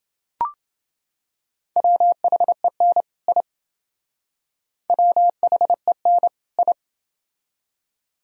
Morse code, heard as a keyed tone, about 28 wpm, 700 Hz. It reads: WHEN I WHEN I